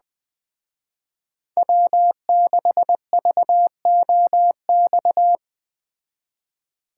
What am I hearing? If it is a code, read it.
Morse code, heard as a keyed tone, about 20 wpm, 700 Hz. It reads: W6VOX
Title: Speed Racing - USA 1x3 Call Signs 17wpm